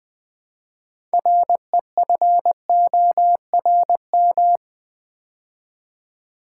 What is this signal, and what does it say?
Morse code, heard as a keyed tone, about 20 wpm, 700 Hz. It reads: REFORM
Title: Build A Word - 2 Letters At A Time - 6-Letter Words - 20wpm